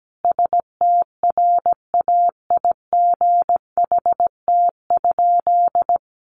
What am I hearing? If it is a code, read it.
Morse code, heard as a keyed tone, about 17 wpm, 700 Hz. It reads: STRAIGHT?
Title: Sentences from Top 1000 Words 17wpm